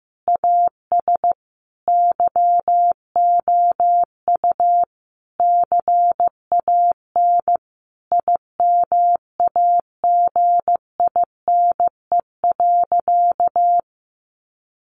Morse code, heard as a keyed tone, about 15 wpm, 700 Hz. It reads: AS YOU CAN IMAGINE.
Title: Sentences from Top 1400 Words 15wpm